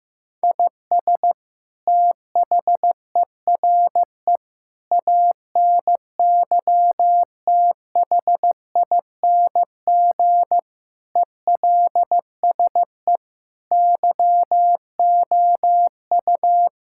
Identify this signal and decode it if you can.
Morse code, heard as a keyed tone, about 15 wpm, 700 Hz. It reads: IS THERE ANYTHING ELSE YOU
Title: Sentences from Top 1600 Words 15wpm